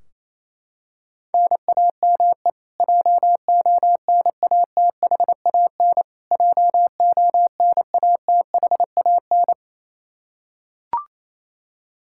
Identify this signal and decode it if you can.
Morse code, heard as a keyed tone, about 28 wpm, 700 Hz. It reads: NAME JONATHAN JONATHAN
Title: Practice QSO - Exchange 1 - Beginning - 28wpm